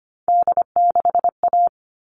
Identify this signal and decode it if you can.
Morse code, heard as a keyed tone, about 25 wpm, 700 Hz. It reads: D6A
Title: All Cut Numbers - 3-Digit 25wpm